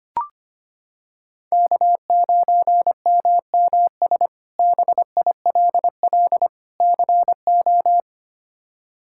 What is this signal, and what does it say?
Morse code, heard as a keyed tone, about 25 wpm, 700 Hz. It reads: K9MMS BILL CO
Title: K1USN SST QSO Practice - 25wpm